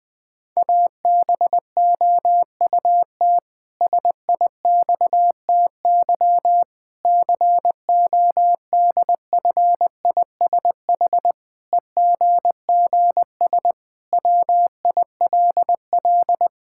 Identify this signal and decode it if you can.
Morse code, heard as a keyed tone, about 21 wpm, 700 Hz. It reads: ABOUT SIXTY CODFISH EGGS WILL
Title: Pangram Challenge v2 - 20wpm